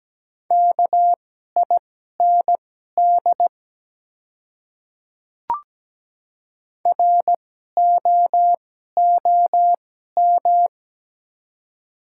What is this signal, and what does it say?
Morse code, heard as a keyed tone, about 17 wpm, spaced as for 12 wpm, 700 Hz. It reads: KIND ROOM
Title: 2x Character Spacing - Top 300 Words - 17wpm